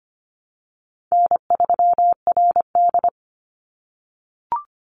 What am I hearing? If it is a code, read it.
Morse code, heard as a keyed tone, about 25 wpm, 700 Hz. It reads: N3RD
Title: Easy Call Signs - 25wpm